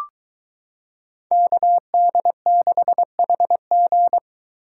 Morse code, heard as a keyed tone, about 23 wpm, 700 Hz. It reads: KD6HG